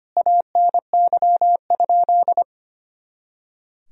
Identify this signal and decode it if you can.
Morse code, heard as a keyed tone, about 25 wpm, 700 Hz. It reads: ANY?
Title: Sentences from Top 300 Words and Abbreviations 25wpm